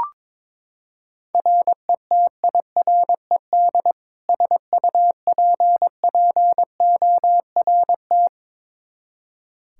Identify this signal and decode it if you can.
Morse code, heard as a keyed tone, about 22 wpm, 700 Hz. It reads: RETIRED SUPPORT
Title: Practice QSO - Exchange 3 - More Info - 22wpm